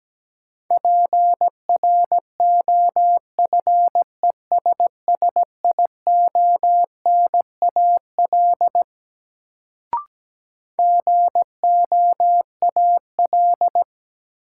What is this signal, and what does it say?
Morse code, heard as a keyed tone, about 17 wpm, 700 Hz. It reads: PROFESSIONAL GOAL
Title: Sentences from Top 1400 Words - New Word Drill Alt - 17wpm